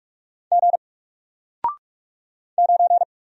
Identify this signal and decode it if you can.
Morse code, heard as a keyed tone, about 46 wpm, 700 Hz. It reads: G 9
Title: Single Letter-Number 45wpm